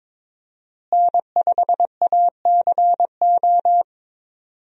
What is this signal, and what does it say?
Morse code, heard as a keyed tone, about 22 wpm, 700 Hz. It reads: N5ACO
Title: USA Call Signs - 5 Characters - Repeated 3x - 22wpm